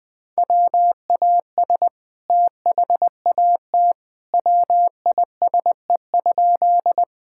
Morse code, heard as a keyed tone, about 20 wpm, 700 Hz. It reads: WAS THAT WISE?
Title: Sentences from Top 1600 Words 20wpm